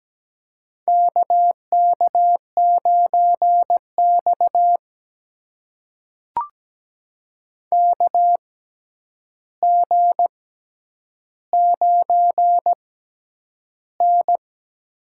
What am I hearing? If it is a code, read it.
Morse code, heard as a keyed tone, about 17 wpm, spaced as for 9 wpm, 700 Hz. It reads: KK9X KG9N
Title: USA Call Signs - 4 Characters - 6x Character Spacing - 17wpm